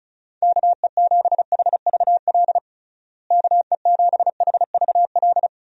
Morse code, heard as a keyed tone, about 34 wpm, 700 Hz. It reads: KE7HVL KE7HVL
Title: USA Call Signs - 6 Characters - Repeated 2x - 35wpm